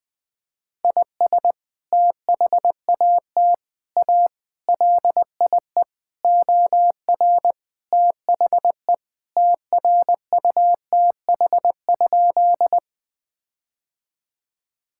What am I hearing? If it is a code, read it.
Morse code, heard as a keyed tone, about 21 wpm, 700 Hz. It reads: IS THAT A LIE OR THE TRUTH?
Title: Sentences from Top 1600 Words 20wpm